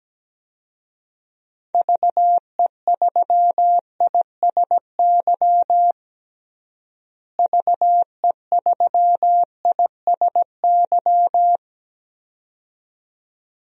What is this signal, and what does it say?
Morse code, heard as a keyed tone, about 17 wpm, 700 Hz. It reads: VE3ISY VE3ISY
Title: DXCC-100 Call Signs - 6 Characters - Repeated 2x - 17wpm